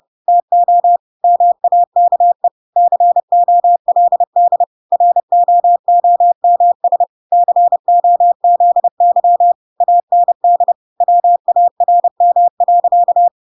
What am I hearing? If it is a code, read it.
Morse code, heard as a keyed tone, about 30 wpm, 700 Hz. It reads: TO MAKE COLD ROOMS COZY AND WARM.
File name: Pinocchio / Pinocchio ch1 30 30 ik1hge